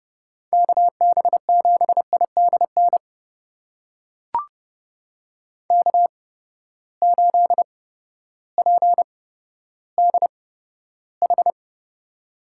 Morse code, heard as a keyed tone, about 31 wpm, spaced as for 14 wpm, 700 Hz. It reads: KB7IDN K8PDH
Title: USA Call Signs - 8x Character Spacing - 30wpm